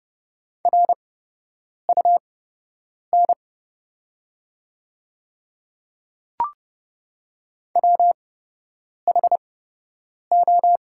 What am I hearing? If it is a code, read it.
Morse code, heard as a keyed tone, about 30 wpm, spaced as for 8 wpm, 700 Hz. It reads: RUN WHO
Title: 8x Character Spacing - Top 500 Words - 30wpm